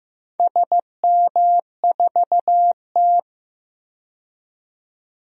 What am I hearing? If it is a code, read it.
Morse code, heard as a keyed tone, about 15 wpm, 700 Hz. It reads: SM4T
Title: DXCC-100 Call Signs - 4 Characters - 15wpm